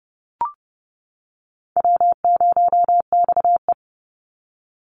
Morse code, heard as a keyed tone, about 30 wpm, 700 Hz. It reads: W0XE
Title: Call Signs - 30wpm